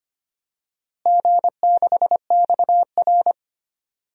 Morse code, heard as a keyed tone, about 25 wpm, 700 Hz. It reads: G6XR
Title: DXCC-100 Call Signs - 4 Characters - Repeated 3x - 25wpm